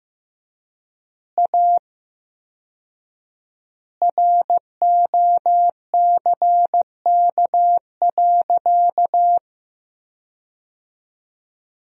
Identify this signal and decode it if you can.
Morse code, heard as a keyed tone, about 16 wpm, 700 Hz. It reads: A ROCK.
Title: 4x Word-Spacing - Sentences from Top 400 Words - 15wpm